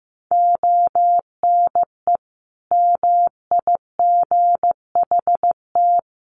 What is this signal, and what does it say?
Morse code, heard as a keyed tone, about 15 wpm, 700 Hz. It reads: ONE MIGHT
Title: Sentences from Top 1400 Words 15wpm